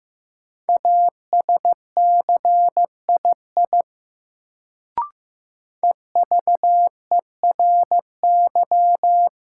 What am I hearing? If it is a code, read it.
Morse code, heard as a keyed tone, about 16 wpm, 700 Hz. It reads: ASCII EVERY
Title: Sets of 5 Words - New Word Drill - 15wpm